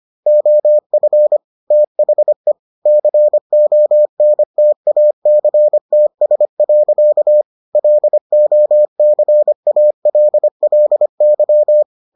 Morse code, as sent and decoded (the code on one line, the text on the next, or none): --- ..-. / - .... / -.-. --- -. - .- -.-. - ... .-.-.- / .-.. --- -.-. .- .-.. .-.. -.--
OF THE CONTACTS. LOCALLY